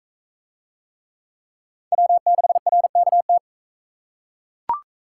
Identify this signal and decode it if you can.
Morse code, heard as a keyed tone, about 41 wpm, 700 Hz. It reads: W6RKT